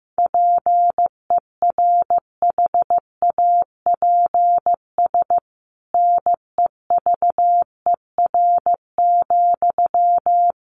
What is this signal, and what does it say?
Morse code, heard as a keyed tone, about 15 wpm, 700 Hz. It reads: PERHAPS NEVER,
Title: Sentences from Top 1200 Words 15wpm